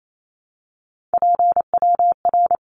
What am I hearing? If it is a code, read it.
Morse code, heard as a keyed tone, about 28 wpm, 700 Hz. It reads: PWR
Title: Abbreviations 28wpm